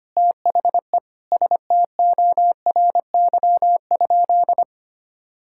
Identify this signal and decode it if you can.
Morse code, heard as a keyed tone, about 25 wpm, 700 Hz. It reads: THE STORY?
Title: Sentences from Top 1000 Words - Review Part 5 - 25wpm